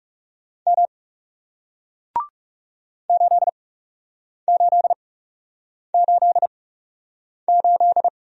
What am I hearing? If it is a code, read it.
Morse code, heard as a keyed tone, about 39 wpm, spaced as for 8 wpm, 700 Hz. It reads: M 8888